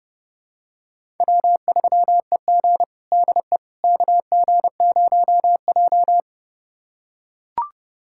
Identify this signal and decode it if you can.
Morse code, heard as a keyed tone, about 31 wpm, 700 Hz. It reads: W3EG DE KG0J
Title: USA Callsign DE Callsign - 4 Characters - 30wpm